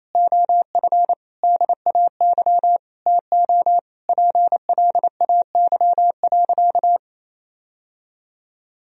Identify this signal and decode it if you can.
Morse code, heard as a keyed tone, about 28 wpm, 700 Hz. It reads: OF DAY TO PLAY.
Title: Sentences from Top 1000 Words - Review Part 1 - 28wpm